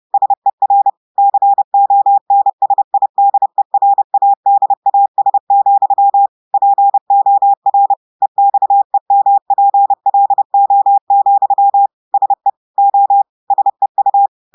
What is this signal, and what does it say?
Morse code, heard as a keyed tone, about 30 wpm, 800 Hz. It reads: SER CONSIDERADAS, POR EXEMPLO, SE O SEU
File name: portugais texte en clair 2 vitesse 30 mots minutes